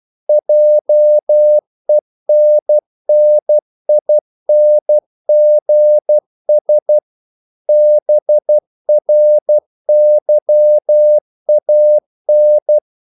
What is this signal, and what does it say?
Morse code, heard as a keyed tone, about 12 wpm, 600 Hz. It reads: JENNINGS BRYAN
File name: CW / Practice / EnglishQuotations0312